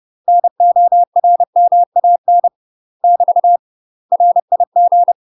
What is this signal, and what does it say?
Morse code, heard as a keyed tone, about 30 wpm, 700 Hz. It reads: NORMAN <BT> RIG
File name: UK MASTER QSOs TRAINING/30WPM